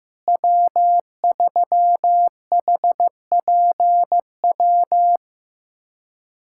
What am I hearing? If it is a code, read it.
Morse code, heard as a keyed tone, about 15 wpm, 700 Hz. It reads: W3HPW
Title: USA Callsign DE Callsign - 5 Characters - 15wpm